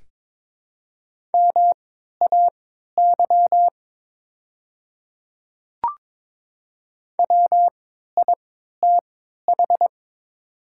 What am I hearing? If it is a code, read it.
Morse code, heard as a keyed tone, about 22 wpm, spaced as for 12 wpm, 700 Hz. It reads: MAY WITH
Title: 3x Character Spacing - Top 500 Words - 22wpm